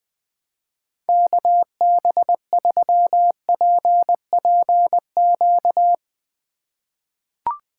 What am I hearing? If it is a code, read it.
Morse code, heard as a keyed tone, about 20 wpm, 700 Hz. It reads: KB3PPQ
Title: USA Call Signs - 6 Characters - 2x Character Spacing - 20wpm